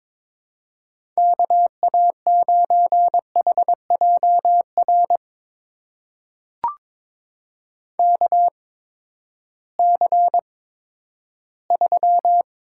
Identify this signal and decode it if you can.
Morse code, heard as a keyed tone, about 21 wpm, spaced as for 12 wpm, 700 Hz. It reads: KA9HJR KC3
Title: USA Call Signs - 6 Characters - 8x Character Spacing - 22wpm